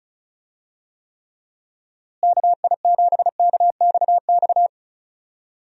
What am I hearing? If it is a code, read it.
Morse code, heard as a keyed tone, about 36 wpm, 700 Hz. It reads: KI7KXX
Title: USA Call Signs - 6 Characters - Repeated 3x - 35wpm